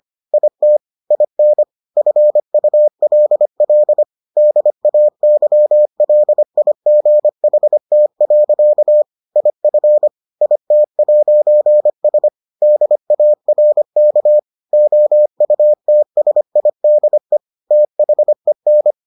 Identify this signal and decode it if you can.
Morse code, heard as a keyed tone, about 25 wpm, 600 Hz. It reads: IT IN FULL DAYLIGHT. IF IT'S DARK OUTSIDE THEN